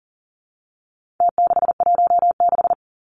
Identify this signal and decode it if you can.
Morse code, heard as a keyed tone, about 40 wpm, 700 Hz. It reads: T616
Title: Numbers - Contest Style - 4-Digit 40wpm